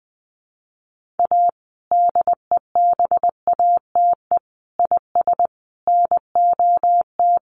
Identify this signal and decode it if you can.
Morse code, heard as a keyed tone, about 20 wpm, 700 Hz. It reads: A DEBATE IS NOT
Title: Sentences from Top 1100 Words 20wpm